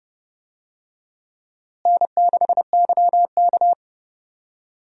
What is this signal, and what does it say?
Morse code, heard as a keyed tone, about 31 wpm, 700 Hz. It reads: N6YK
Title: USA Call Signs - 4 Characters - Repeated 3x - 30wpm